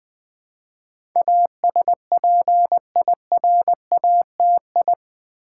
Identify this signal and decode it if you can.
Morse code, heard as a keyed tone, about 20 wpm, 700 Hz. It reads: ASPIRATI
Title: Build A Word - 4 Letters At A Time - 10-Letter Words - 20wpm